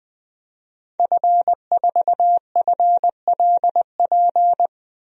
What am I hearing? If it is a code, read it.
Morse code, heard as a keyed tone, about 20 wpm, 700 Hz. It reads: F4FLP
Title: DXCC-100 Call Signs - 5 Characters - Repeated 3x - 20wpm